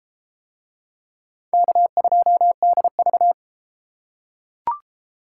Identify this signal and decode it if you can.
Morse code, heard as a keyed tone, about 31 wpm, 700 Hz. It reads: K2DV